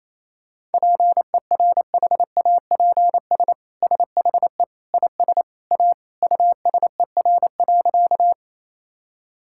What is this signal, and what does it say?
Morse code, heard as a keyed tone, about 28 wpm, 700 Hz. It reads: PERHAPS SHE IS A USER.
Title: Sentences from Top 1000 Words - Review Part 5 - 28wpm